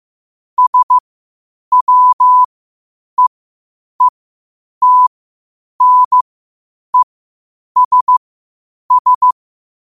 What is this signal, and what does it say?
Morse code, heard as a keyed tone, about 15 wpm, spaced as for 8 wpm, 1000 Hz. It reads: SWEETNESS
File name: qotdfiles / 8wpm